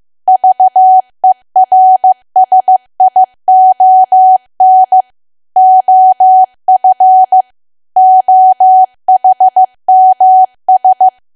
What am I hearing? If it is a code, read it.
Morse code, heard as a keyed tone, about 15 wpm, 750 Hz. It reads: VERSION OF OHMS
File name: cwa / W1AW-Level2 / 15WPM